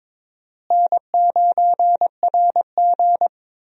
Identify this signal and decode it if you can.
Morse code, heard as a keyed tone, about 22 wpm, 700 Hz. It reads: N9RG
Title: USA Call Signs - 4 Characters - Repeated 3x - 22wpm